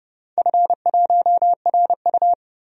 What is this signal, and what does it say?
Morse code, heard as a keyed tone, about 30 wpm, 700 Hz. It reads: F1RU
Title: DXCC-100 Call Signs - 4 Characters - 30wpm